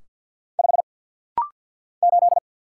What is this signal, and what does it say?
Morse code, heard as a keyed tone, about 50 wpm, 700 Hz. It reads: 5 8
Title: Single Letter-Number - Rapid-Fire 50wpm